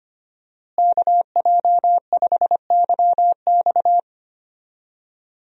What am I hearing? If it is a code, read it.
Morse code, heard as a keyed tone, about 25 wpm, 700 Hz. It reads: KJ5YX
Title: USA Call Signs - 5 Characters - Repeated 2x - 25wpm